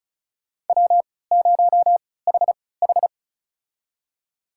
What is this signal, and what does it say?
Morse code, heard as a keyed tone, about 34 wpm, spaced as for 20 wpm, 700 Hz. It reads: W0HH
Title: USA Call Signs - 4 Characters - 3x Character Spacing - 35wpm